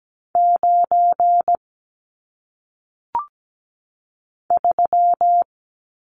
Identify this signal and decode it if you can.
Morse code, heard as a keyed tone, about 17 wpm, 700 Hz. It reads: 9 3
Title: Numbers - 1-Digit 17wpm